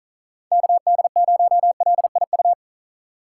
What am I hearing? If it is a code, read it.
Morse code, heard as a keyed tone, about 42 wpm, 700 Hz. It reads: KD0LIU
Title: Speed Racing - USA 2x3 Call Signs 35wpm